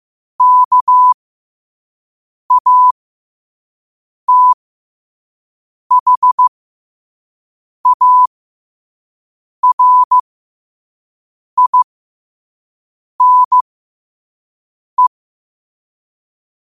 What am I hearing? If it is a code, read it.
Morse code, heard as a keyed tone, about 15 wpm, spaced as for 5 wpm, 1000 Hz. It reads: KATHARINE
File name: qotdfiles / 5wpm